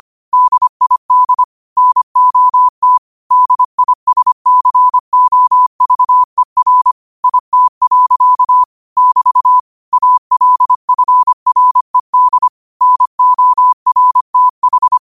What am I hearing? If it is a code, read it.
Morse code, heard as a keyed tone, about 25 wpm, 1000 Hz. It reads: DID NOT DISCOVER IT. <BT> ALFRED NORTH
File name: qotdfiles / 25wpm